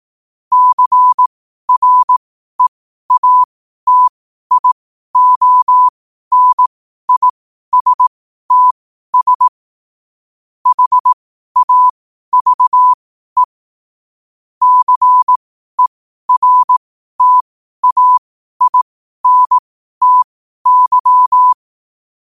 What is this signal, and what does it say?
Morse code, heard as a keyed tone, about 18 wpm, spaced as for 12 wpm, 1000 Hz. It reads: CREATIONISTS HAVE CERTAINTY
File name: qotdfiles / 12wpm